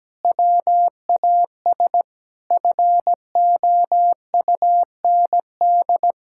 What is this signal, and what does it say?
Morse code, heard as a keyed tone, about 17 wpm, 700 Hz. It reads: WAS FOUND
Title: Sentences from Top 1600 Words 17wpm